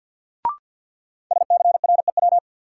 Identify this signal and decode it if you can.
Morse code, heard as a keyed tone, about 52 wpm, 700 Hz. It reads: SCREW
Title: Sentences from Top 1600 Words - New Word Drill - 50wpm